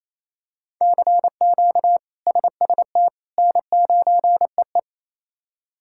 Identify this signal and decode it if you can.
Morse code, heard as a keyed tone, about 28 wpm, 700 Hz. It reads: CQ SST N9EE
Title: K1USN SST QSO Practice - 28wpm